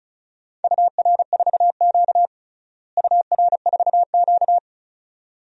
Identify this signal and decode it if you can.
Morse code, heard as a keyed tone, about 36 wpm, 700 Hz. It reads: UR4Q UR4Q
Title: DXCC-100 Call Signs - 4 Characters - Repeated 2x - 35wpm